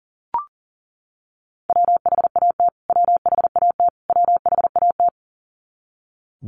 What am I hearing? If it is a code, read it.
Morse code, heard as a keyed tone, about 40 wpm, 700 Hz. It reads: WHAT WHAT WHAT
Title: Top 200 Words Repeated 40wpm